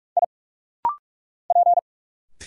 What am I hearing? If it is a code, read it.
Morse code, heard as a keyed tone, about 46 wpm, 700 Hz. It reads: I P
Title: Single Letters - Rapid-Fire 45wpm